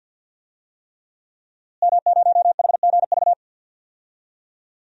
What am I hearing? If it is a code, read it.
Morse code, heard as a keyed tone, about 52 wpm, 700 Hz. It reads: M0HGV
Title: DXCC-100 Call Signs - 5 Characters - Repeated 3x - 50wpm